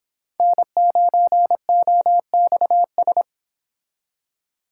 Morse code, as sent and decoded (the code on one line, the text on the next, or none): -. ----. --- -..- ...
N9OXS